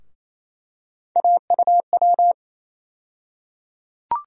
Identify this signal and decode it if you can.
Morse code, heard as a keyed tone, about 28 wpm, 700 Hz. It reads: AUW
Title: USA Call Sign Postfixes - 3 Letters 28wpm